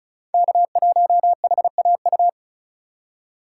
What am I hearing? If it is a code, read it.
Morse code, heard as a keyed tone, about 36 wpm, 700 Hz. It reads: K1HAU
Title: USA Call Signs - 5 Characters - 2x Character Spacing - 35wpm